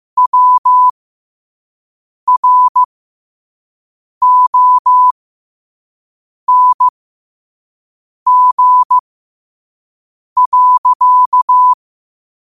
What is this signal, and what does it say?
Morse code, heard as a keyed tone, about 15 wpm, spaced as for 5 wpm, 1000 Hz. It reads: WRONG.
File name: qotdfiles / 5wpm